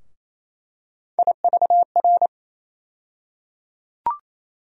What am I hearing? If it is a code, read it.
Morse code, heard as a keyed tone, about 28 wpm, 700 Hz. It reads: IVR
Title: USA Call Sign Postfixes - 3 Letters 28wpm